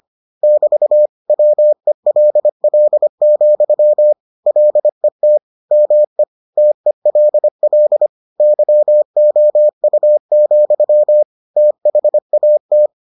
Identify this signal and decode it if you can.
Morse code, heard as a keyed tone, about 25 wpm, 600 Hz. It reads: <BT> WELL, LET ME TELL YOU, THAT